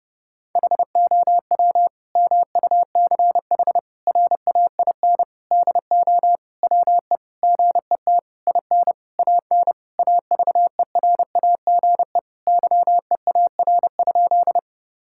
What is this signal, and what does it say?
Morse code, heard as a keyed tone, about 30 wpm, 700 Hz. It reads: HOW MUCH RAIN DO WE GET IN AN AVERAGE YEAR?
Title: Sentences from Top 1000 Words - Review Part 6 - 30wpm